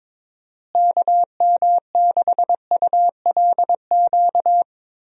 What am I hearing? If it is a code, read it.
Morse code, heard as a keyed tone, about 22 wpm, 700 Hz. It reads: KM6ULQ
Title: USA Call Signs - 6 Characters - Repeated 3x - 22wpm